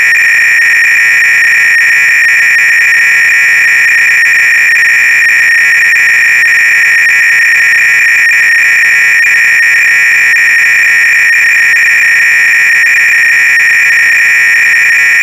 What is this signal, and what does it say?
300-baud modem data.
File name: ACR-Audio-Cassette-Programs / CharlesSwiger-Donations / KING cload-k